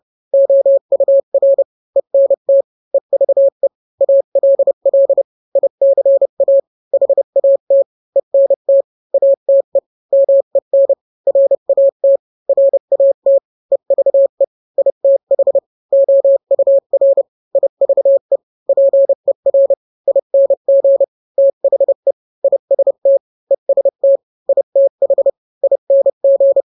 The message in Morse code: OUR ENT EVE ALL ICA HAT ENT ATE MEN RAT RAT EVE ITH OUR IVE PER ING THE IST EST ITH ING